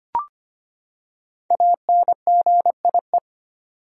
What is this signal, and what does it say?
Morse code, heard as a keyed tone, about 25 wpm, 700 Hz. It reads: ANGIE
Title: USA Common Names - Max 5 Letters - 25wpm